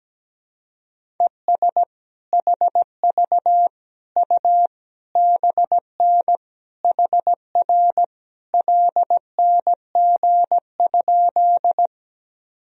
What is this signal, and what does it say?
Morse code, heard as a keyed tone, about 17 wpm, 700 Hz. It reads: ES HV U BN HR LNG?
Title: Sentences from Top 200 Words and Abbreviations 17wpm